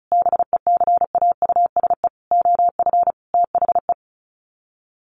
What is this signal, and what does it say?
Morse code, heard as a keyed tone, about 34 wpm, 700 Hz. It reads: BECAUSE OF THE
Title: Sets of 5 Words Encore2 35wpm